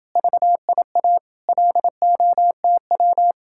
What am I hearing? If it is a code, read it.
Morse code, heard as a keyed tone, about 29 wpm, 700 Hz. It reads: VIA LOTW